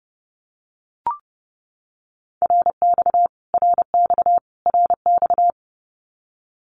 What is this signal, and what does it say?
Morse code, heard as a keyed tone, about 30 wpm, 700 Hz. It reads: RX RX RX
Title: Abbreviations Repeated 3x - 30wpm